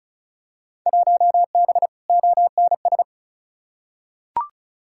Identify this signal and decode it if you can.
Morse code, heard as a keyed tone, about 36 wpm, 700 Hz. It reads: 1B ONS
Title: Field Day Exchanges Encore - 35wpm